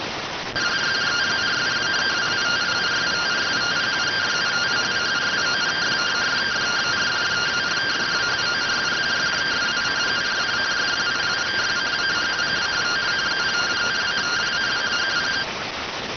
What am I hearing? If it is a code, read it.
Two-tone teleprinter signal, about 170 Hz shift.